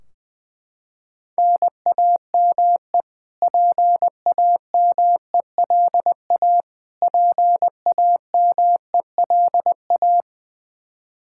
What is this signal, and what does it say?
Morse code, heard as a keyed tone, about 20 wpm, 700 Hz. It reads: NAME PAMELA PAMELA
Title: Practice QSO - Exchange 1 - Beginning - 20wpm